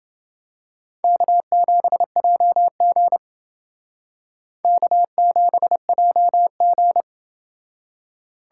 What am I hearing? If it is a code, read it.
Morse code, heard as a keyed tone, about 28 wpm, 700 Hz. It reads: K7JG K7JG